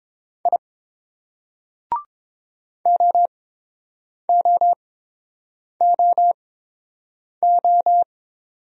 Morse code, heard as a keyed tone, about 30 wpm, spaced as for 7 wpm, 700 Hz. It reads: I OOOO